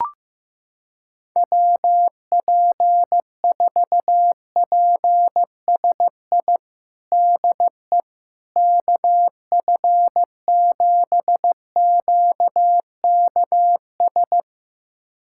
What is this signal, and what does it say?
Morse code, heard as a keyed tone, about 15 wpm, 700 Hz. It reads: WP4PSI DE KF7QKS
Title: USA Callsign DE Callsign - 6 Characters - 15wpm